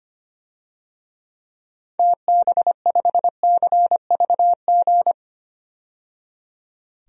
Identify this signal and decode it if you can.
Morse code, heard as a keyed tone, about 25 wpm, 700 Hz. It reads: TB5CVG
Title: DXCC-100 Call Signs - 6 Characters - 25wpm